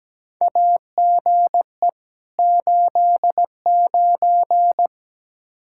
Morse code, heard as a keyed tone, about 17 wpm, 700 Hz. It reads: AGE 89
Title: K1USN SST QSO Practice 17wpm